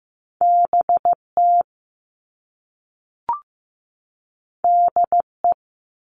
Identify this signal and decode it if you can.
Morse code, heard as a keyed tone, about 15 wpm, 700 Hz. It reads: BT DE
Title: All Cut Numbers - 2-Digit 15wpm